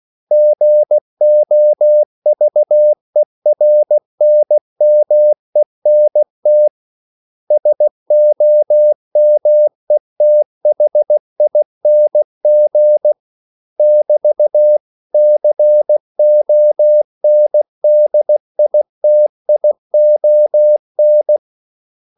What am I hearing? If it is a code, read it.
Morse code, heard as a keyed tone, about 16 wpm, 600 Hz. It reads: GOVERNMENT SOMETHING <BT>CONDITION
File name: CW / Practice / 500MostCommonEnglishWords16